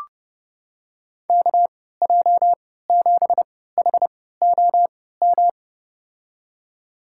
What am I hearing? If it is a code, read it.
Morse code, heard as a keyed tone, about 30 wpm, spaced as for 17 wpm, 700 Hz. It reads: KJ7HOM
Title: USA Call Signs - 3x Character Spacing - 30wpm